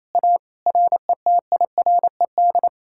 Morse code, heard as a keyed tone, about 37 wpm, 700 Hz. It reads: A RETIRED